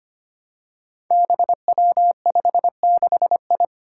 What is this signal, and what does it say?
Morse code, heard as a keyed tone, about 25 wpm, 700 Hz. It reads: BW56I
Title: Five Challenging Letter-Number Combinations 25wpm